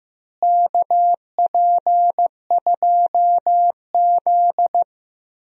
Morse code, heard as a keyed tone, about 15 wpm, 700 Hz. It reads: KP2Z
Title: DXCC-100 Call Signs - 4 Characters - Repeated 3x - 15wpm